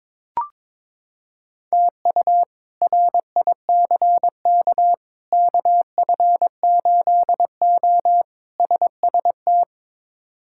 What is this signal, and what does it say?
Morse code, heard as a keyed tone, about 21 wpm, 700 Hz. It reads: TU RICK KF8O SST
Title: K1USN SST QSO Practice - 22wpm